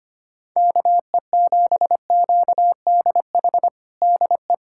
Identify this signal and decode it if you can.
Morse code, heard as a keyed tone, about 25 wpm, 700 Hz. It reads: KE7QDH DE
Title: USA Callsign DE Callsign - 6 Characters - 25wpm